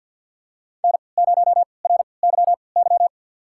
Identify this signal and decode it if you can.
Morse code, heard as a keyed tone, about 50 wpm, spaced as for 28 wpm, 700 Hz. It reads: N0RYY
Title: USA Call Signs - 5 Characters - 3x Character Spacing - 50wpm